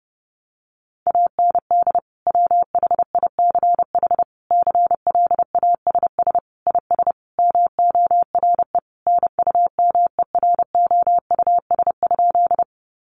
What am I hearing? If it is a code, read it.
Morse code, heard as a keyed tone, about 30 wpm, 700 Hz. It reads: AND WHICH CLASS IS MORE NUMEROUS?
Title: Sentences from Top 1400 Words 30wpm